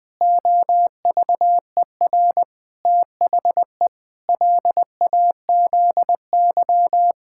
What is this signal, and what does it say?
Morse code, heard as a keyed tone, about 20 wpm, 700 Hz. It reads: OVER THE LAZY
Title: Pangram Challenge v2 - 20wpm